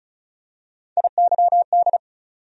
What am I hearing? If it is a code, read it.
Morse code, heard as a keyed tone, about 36 wpm, 700 Hz. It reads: IYD
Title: USA Call Sign Postfixes - 3 Letters 35wpm